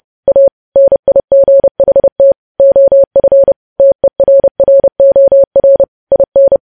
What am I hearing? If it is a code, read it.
Morse code, heard as a keyed tone, about 30 wpm, 550 Hz. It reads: A NIGHT OF TERROR IN
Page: news headlines in morse code at 30 wpm